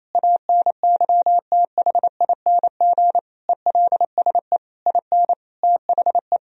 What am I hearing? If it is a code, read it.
Morse code, heard as a keyed tone, about 28 wpm, 700 Hz. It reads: ANYTHING ELSE IN THE
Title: Sentences from Top 1000 Words - Review Part 4 - 28wpm